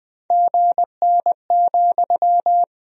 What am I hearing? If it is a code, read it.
Morse code, heard as a keyed tone, about 21 wpm, 700 Hz. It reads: GN,
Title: Sentences from Top 300 Words and Abbreviations 20wpm